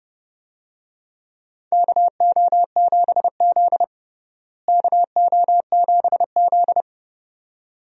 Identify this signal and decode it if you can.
Morse code, heard as a keyed tone, about 32 wpm, 700 Hz. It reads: KO7Z KO7Z
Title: USA Call Signs - 4 Characters - Repeated 2x - 30wpm